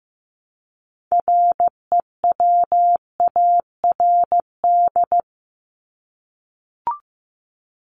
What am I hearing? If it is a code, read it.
Morse code, heard as a keyed tone, about 15 wpm, 700 Hz. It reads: REWARD
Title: Top 1751-2000 Words 15wpm